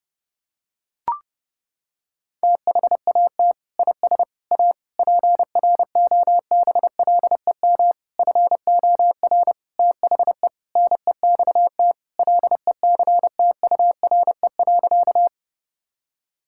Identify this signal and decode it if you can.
Morse code, heard as a keyed tone, about 30 wpm, 700 Hz. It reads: THAT IS A PROBLEM FOR THE NEXT LECTURE.
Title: Sentences from Top 1600 Words 30wpm